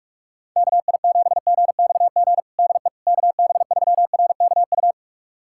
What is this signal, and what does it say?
Morse code, heard as a keyed tone, about 41 wpm, 700 Hz. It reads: KI7GXG DE KB3RKU
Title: USA Callsign DE Callsign 45wpm